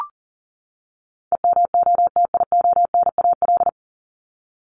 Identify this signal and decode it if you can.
Morse code, heard as a keyed tone, about 40 wpm, 700 Hz. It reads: EMOTIONAL
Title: Top 501-750 Words 40wpm